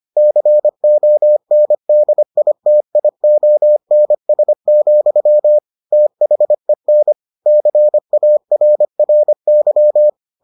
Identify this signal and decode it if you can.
Morse code, heard as a keyed tone, about 25 wpm, 600 Hz. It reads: CONDITIONS, THEN CARRY